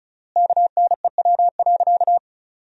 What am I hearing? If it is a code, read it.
Morse code, heard as a keyed tone, about 36 wpm, 700 Hz. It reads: KNEW.
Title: 3x Word-Spacing - Sentences from Top 400 Words - 35wpm